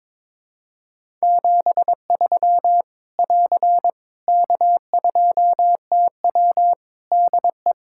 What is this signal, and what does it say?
Morse code, heard as a keyed tone, about 22 wpm, 700 Hz. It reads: 73 <AR> K2TW DE
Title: Practice QSO - Exchange 4 - Ending - 22wpm